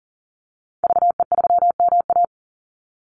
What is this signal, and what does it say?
Morse code, heard as a keyed tone, about 40 wpm, 700 Hz. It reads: VE3MA
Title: Call Signs - 40wpm